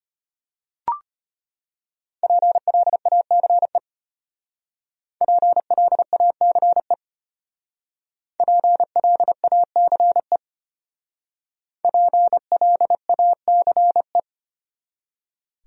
Morse code, heard as a keyed tone, about 34 wpm, 700 Hz. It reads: PLACE PLACE PLACE PLACE